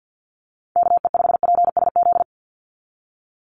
Morse code, heard as a keyed tone, about 48 wpm, 700 Hz. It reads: KE5PSZ
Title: USA Call Signs 50wpm